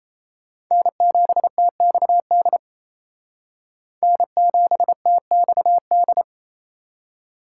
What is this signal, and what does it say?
Morse code, heard as a keyed tone, about 37 wpm, 700 Hz. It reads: N7TXD N7TXD